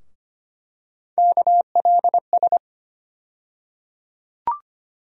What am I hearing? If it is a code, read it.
Morse code, heard as a keyed tone, about 25 wpm, 700 Hz. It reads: KLS
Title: USA Call Sign Postfixes - 3 Letters 25wpm